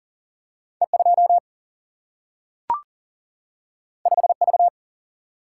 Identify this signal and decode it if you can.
Morse code, heard as a keyed tone, about 40 wpm, 700 Hz. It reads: E 2 5 V
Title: Single-Letter Single-Number 40wpm